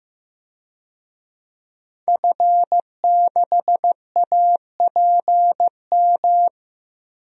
Morse code, heard as a keyed tone, about 15 wpm, 700 Hz. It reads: F6APM
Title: DXCC-100 Call Signs - 5 Characters - Repeated 3x - 15wpm